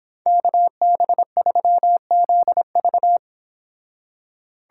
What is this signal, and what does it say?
Morse code, heard as a keyed tone, about 26 wpm, 700 Hz. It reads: KB3ZV
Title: Speed Racing - USA 2x2 Call Signs 22wpm